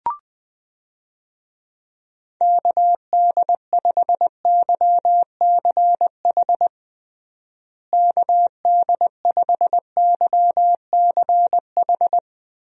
Morse code, heard as a keyed tone, about 21 wpm, 700 Hz. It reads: KD5YCH KD5YCH
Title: USA Call Signs - 6 Characters - Repeated 3x - 20wpm